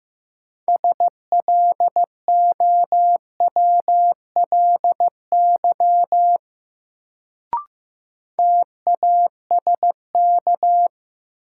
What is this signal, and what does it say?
Morse code, heard as a keyed tone, about 15 wpm, 700 Hz. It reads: SLOWLY TASK
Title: Sentences from Top 1100 Words - New Word Drill Alt - 15wpm